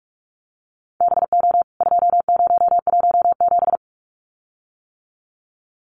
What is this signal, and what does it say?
Morse code, heard as a keyed tone, about 44 wpm, 700 Hz. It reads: BO 2017
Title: CWT Exchange 45wpm